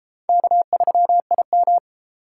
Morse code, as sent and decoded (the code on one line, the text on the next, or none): -.- ...-- .. --
K3IM